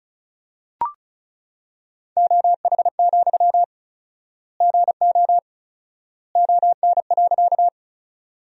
Morse code, heard as a keyed tone, about 36 wpm, 700 Hz. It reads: OH, GO ON.
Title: 4x Word-Spacing - Sentences from Top 500 Words - 35wpm